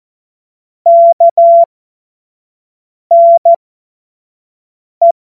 Morse code, heard as a keyed tone, about 14 wpm, spaced as for 5 wpm, 680 Hz. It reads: KNE
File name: Witz-16-September-2025-25-F70